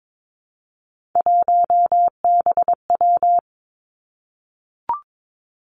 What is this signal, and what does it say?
Morse code, heard as a keyed tone, about 22 wpm, 700 Hz. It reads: 1BW
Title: Three Challenging Letter-Number Combinations 22wpm